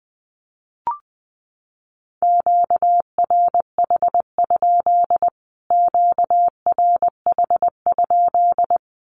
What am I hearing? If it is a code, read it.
Morse code, heard as a keyed tone, about 20 wpm, 700 Hz. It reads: QRH? QRH?
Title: Q-Codes Repeated 3x 20wpm